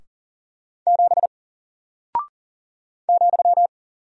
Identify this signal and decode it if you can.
Morse code, heard as a keyed tone, about 40 wpm, 700 Hz. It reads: 7 ,
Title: Single Character (Letter:Number:Punctuation:Prosign) Alt 40wpm